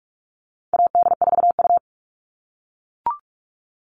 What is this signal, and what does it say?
Morse code, heard as a keyed tone, about 44 wpm, 700 Hz. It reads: AD4U
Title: All Cut Numbers - 4-Digit 45wpm